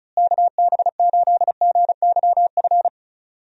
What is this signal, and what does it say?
Morse code, heard as a keyed tone, about 36 wpm, 700 Hz. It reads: KB8GYF
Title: USA Call Signs - 6 Characters - Repeated 3x - 35wpm